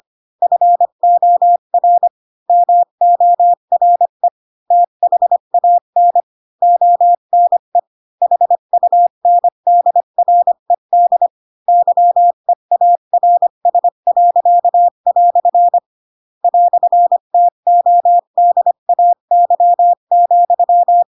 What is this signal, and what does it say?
Morse code, heard as a keyed tone, about 25 wpm, 700 Hz. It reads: FOR MORE THAN ONE HUNDRED YEARS." "TODAY,
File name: Pinocchio / Pinocchio ch4 20 25 ik1hge